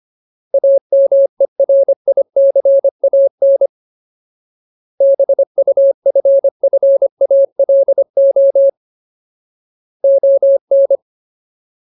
Morse code, heard as a keyed tone, about 25 wpm, 550 Hz. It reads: AMERICAN BUFFALO ON
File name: Ultimate Word List / 25WPM Ultimate